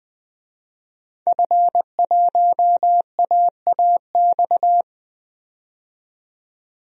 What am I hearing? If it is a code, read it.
Morse code, heard as a keyed tone, about 20 wpm, 700 Hz. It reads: F1AAX
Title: DXCC-100 Call Signs - 5 Characters - 20wpm